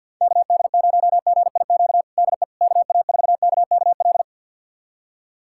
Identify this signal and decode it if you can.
Morse code, heard as a keyed tone, about 46 wpm, 700 Hz. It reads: KD0GIQ DE KA4KKL
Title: USA Callsign DE Callsign - 6 Characters - 50wpm